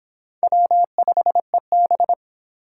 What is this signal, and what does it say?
Morse code, heard as a keyed tone, about 30 wpm, 700 Hz. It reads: W5EB